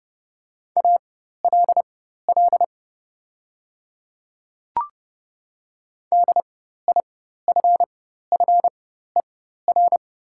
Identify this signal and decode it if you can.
Morse code, heard as a keyed tone, about 30 wpm, spaced as for 14 wpm, 700 Hz. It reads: ALL DIFFER
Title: 4x Character Spacing - Top 500 Words - 30wpm